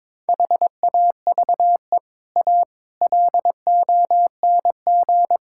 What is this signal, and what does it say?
Morse code, heard as a keyed tone, about 22 wpm, 700 Hz. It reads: HAVE A LONG
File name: practice / 22wpm / Sets of 6 Words - 22wpm